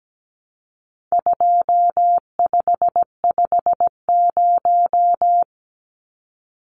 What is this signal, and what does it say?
Morse code, heard as a keyed tone, about 17 wpm, 700 Hz. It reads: 2550
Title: Numbers - 4-Digit 17wpm